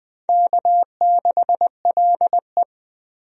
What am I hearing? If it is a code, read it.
Morse code, heard as a keyed tone, about 20 wpm, 700 Hz. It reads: K6LE
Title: USA Call Signs - 4 Characters - 20wpm